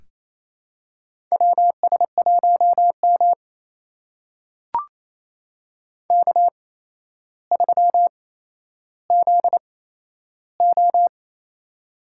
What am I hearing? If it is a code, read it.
Morse code, heard as a keyed tone, about 28 wpm, spaced as for 12 wpm, 700 Hz. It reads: WS1M K3ZO
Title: USA Call Signs - 4 Characters - 8x Character Spacing - 28wpm